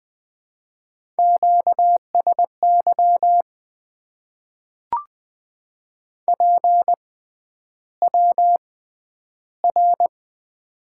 Morse code, heard as a keyed tone, about 20 wpm, spaced as for 10 wpm, 700 Hz. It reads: QSY PWR
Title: Top 90 QSO Elements - 6x Character Spacing - 20wpm